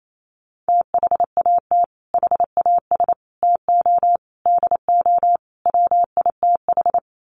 Morse code, heard as a keyed tone, about 28 wpm, 700 Hz. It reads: THAT HAS TO DO WITH
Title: Sets of 5 Words Encore2 28wpm